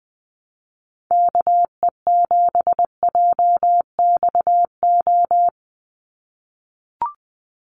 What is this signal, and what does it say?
Morse code, heard as a keyed tone, about 20 wpm, 700 Hz. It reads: KE7JXO
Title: USA Call Signs 20wpm